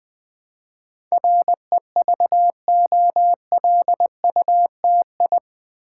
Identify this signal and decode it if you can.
Morse code, heard as a keyed tone, about 20 wpm, 700 Hz. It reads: REVOLUTI
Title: Build A Word - 4 Letters At A Time - 10-Letter Words - 20wpm